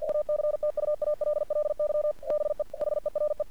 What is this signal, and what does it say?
Morse code, heard as a keyed tone, about 49 wpm, 600 Hz. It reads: ACTUALLY BE HERE
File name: archived / CRSnet